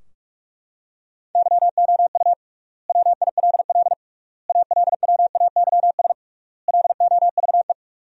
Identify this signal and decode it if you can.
Morse code, heard as a keyed tone, about 41 wpm, 700 Hz. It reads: YOU WILL ALWAYS LOVE
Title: 3x Word-Spacing - Sentences from Top 400 Words - 45wpm